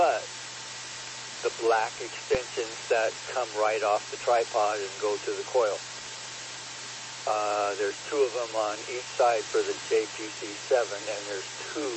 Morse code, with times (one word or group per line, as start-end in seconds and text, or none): none